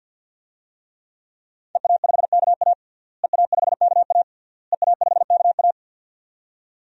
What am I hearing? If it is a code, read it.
Morse code, heard as a keyed tone, about 53 wpm, 700 Hz. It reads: EA5KA EA5KA EA5KA
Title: DXCC-100 Call Signs - 5 Characters - Repeated 3x - 50wpm